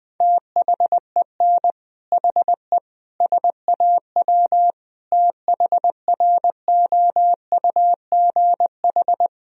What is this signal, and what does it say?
Morse code, heard as a keyed tone, about 20 wpm, 700 Hz. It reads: THEN HE SAW THROUGH
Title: Sentences from Top 1000 Words - Review Part 1 - 20wpm